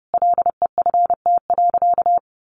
Morse code, heard as a keyed tone, about 30 wpm, 700 Hz. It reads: LEFT.
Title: Sentences from Top 300 Words 30wpm